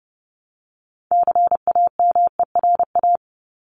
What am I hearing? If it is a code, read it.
Morse code, heard as a keyed tone, about 30 wpm, 700 Hz. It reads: CAMERA
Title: Top 501-750 Words 30wpm